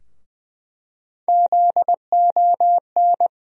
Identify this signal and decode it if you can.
Morse code, heard as a keyed tone, about 20 wpm, 700 Hz. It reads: ZON
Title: USA Call Sign Postfixes - 3 Letters 20wpm